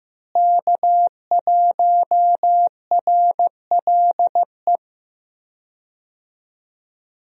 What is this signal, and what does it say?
Morse code, heard as a keyed tone, about 15 wpm, 700 Hz. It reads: K1RLE
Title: USA Call Signs - 5 Characters - Repeated 3x - 15wpm